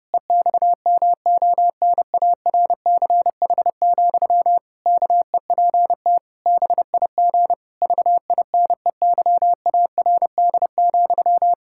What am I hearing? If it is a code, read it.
Morse code, heard as a keyed tone, about 30 wpm, 700 Hz. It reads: EXMONARCH, KEPT BIG VINEYARD,
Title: Pangram Challenge v2 - 30wpm